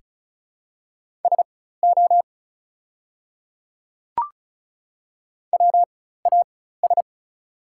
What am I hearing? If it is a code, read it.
Morse code, heard as a keyed tone, about 34 wpm, spaced as for 16 wpm, 700 Hz. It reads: SO WAS
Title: 4x Character Spacing - Top 100 Words - 35wpm